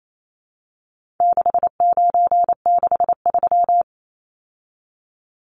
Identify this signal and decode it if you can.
Morse code, heard as a keyed tone, about 28 wpm, 700 Hz. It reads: 6963
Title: Numbers - 4-Digit 28wpm